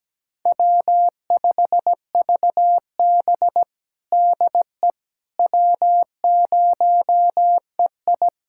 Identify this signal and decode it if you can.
Morse code, heard as a keyed tone, about 17 wpm, 700 Hz. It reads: W5VB DE W0EI
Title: USA Callsign DE Callsign - 4 Characters - 17wpm